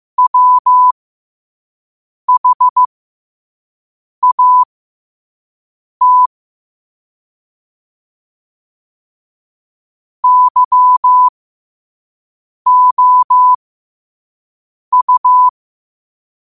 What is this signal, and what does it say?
Morse code, heard as a keyed tone, about 15 wpm, spaced as for 5 wpm, 1000 Hz. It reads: WHAT YOU
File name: qotdfiles / 5wpm